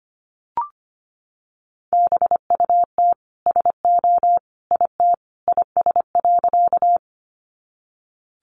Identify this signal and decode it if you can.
Morse code, heard as a keyed tone, about 25 wpm, 700 Hz. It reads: BUT SO IT IS.
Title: Sentences from Top 100 Words 25wpm